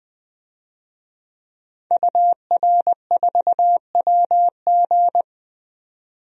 Morse code, heard as a keyed tone, about 20 wpm, 700 Hz. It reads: UR4WG
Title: DXCC-100 Call Signs - 5 Characters - 20wpm